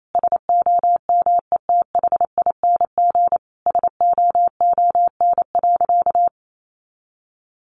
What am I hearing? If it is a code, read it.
Morse code, heard as a keyed tone, about 28 wpm, 700 Hz. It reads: SOMETHING SOON.
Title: Sentences from Top 1500 Words 28wpm